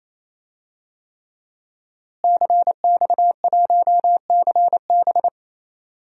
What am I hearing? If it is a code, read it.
Morse code, heard as a keyed tone, about 28 wpm, 700 Hz. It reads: CX1CB
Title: DXCC-100 Call Signs - 5 Characters - Repeated 3x - 28wpm